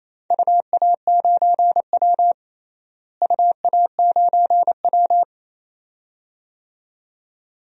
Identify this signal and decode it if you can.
Morse code, heard as a keyed tone, about 28 wpm, 700 Hz. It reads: UA9W UA9W
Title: DXCC-100 Call Signs - 4 Characters - Repeated 2x - 28wpm